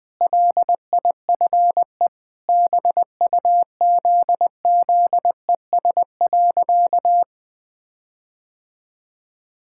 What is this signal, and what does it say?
Morse code, heard as a keyed tone, about 20 wpm, 700 Hz. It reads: LIFE BUZZES.
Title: Sentences with 4-Word Phrases - 20wpm